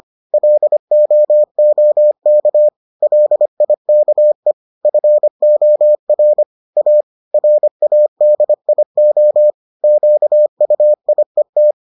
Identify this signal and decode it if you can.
Morse code, heard as a keyed tone, about 25 wpm, 600 Hz. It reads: LOOK LIKE FOR A RADIO QUIET